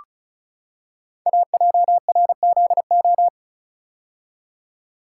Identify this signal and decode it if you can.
Morse code, heard as a keyed tone, about 36 wpm, 700 Hz. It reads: AJRZO
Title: Code Groups - 5 Letter:Numbers - 35wpm